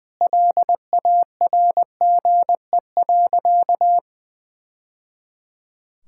Morse code, heard as a keyed tone, about 20 wpm, 700 Hz. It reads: LARGE.
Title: Sentences from Top 1000 Words - Review Part 5 - 20wpm